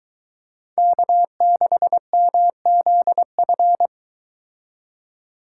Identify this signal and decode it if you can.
Morse code, heard as a keyed tone, about 23 wpm, 700 Hz. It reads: K6MZF